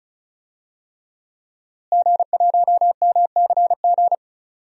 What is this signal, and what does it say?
Morse code, heard as a keyed tone, about 34 wpm, 700 Hz. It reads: G1MCG
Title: DXCC-100 Call Signs - 5 Characters - Repeated 3x - 35wpm